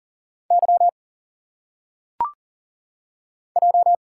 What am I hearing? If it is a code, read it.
Morse code, heard as a keyed tone, about 40 wpm, 700 Hz. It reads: Y J
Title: Single Letters - Phonetic Pronunciation 40wpm